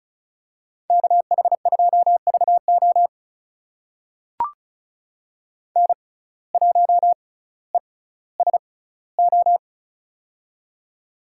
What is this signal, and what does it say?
Morse code, heard as a keyed tone, about 36 wpm, spaced as for 18 wpm, 700 Hz. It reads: KH2VO N1ESO
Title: USA Call Signs - 5 Characters - 6x Character Spacing - 35wpm